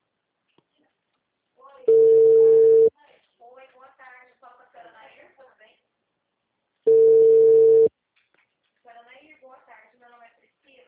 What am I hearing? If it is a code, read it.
Ringback tone (China, 450 Hz), with speech.
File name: out- -2009